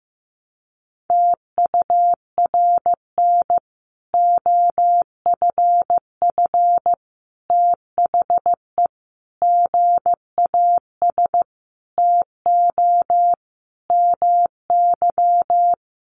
Morse code, heard as a keyed tone, about 15 wpm, 700 Hz. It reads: TURN OFF THE GAS TO MY
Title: Sentences from Top 1100 Words 15wpm